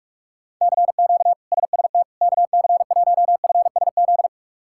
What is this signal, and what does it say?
Morse code, heard as a keyed tone, about 41 wpm, 700 Hz. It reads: CQ SST KC1FSZ
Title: K1USN SST QSO Practice - 45wpm